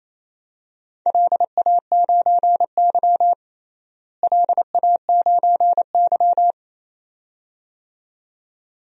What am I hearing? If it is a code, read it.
Morse code, heard as a keyed tone, about 28 wpm, 700 Hz. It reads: LA9Y LA9Y
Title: DXCC-100 Call Signs - 4 Characters - Repeated 2x - 28wpm